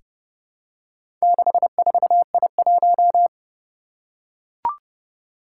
Morse code, heard as a keyed tone, about 30 wpm, 700 Hz. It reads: 64I1
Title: Four Challenging Letter-Number Combinations 30wpm